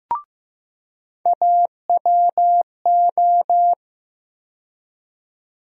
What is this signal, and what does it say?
Morse code, heard as a keyed tone, about 15 wpm, 700 Hz. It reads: AWO
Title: USA Call Sign Postfixes - 3 Letters 15wpm